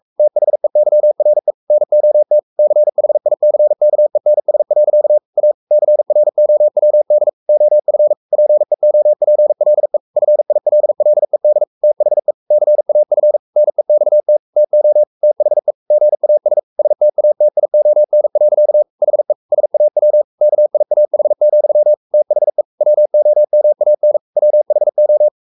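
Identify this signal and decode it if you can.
Morse code, heard as a keyed tone, about 44 wpm, 600 Hz. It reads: THEYRE NOT CHICKENS. A CROWD OF PEOPLE FILLED THE CAF NEXT TO THE GAS STATION. HE SAW KIAH, THE WOMAN WHO
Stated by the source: Morse code at 43 wpm sending Ken Follett's novel Never in English, from CW med Gnister.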